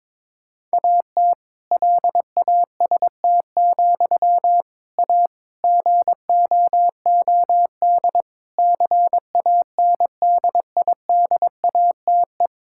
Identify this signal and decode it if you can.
Morse code, heard as a keyed tone, about 22 wpm, 700 Hz. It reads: AT LAST, A GOOD CANDIDATE
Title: Sentences from Top 1600 Words 22wpm